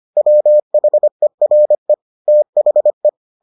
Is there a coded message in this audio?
Yes: Morse code from 0.2 to 3.1 s, heard as a keyed tone, about 25 wpm, 600 Hz. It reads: WHERE THE